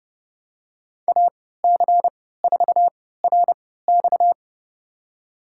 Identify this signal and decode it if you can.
Morse code, heard as a keyed tone, about 31 wpm, spaced as for 17 wpm, 700 Hz. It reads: AC4RX
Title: USA Call Signs - 5 Characters - 3x Character Spacing - 30wpm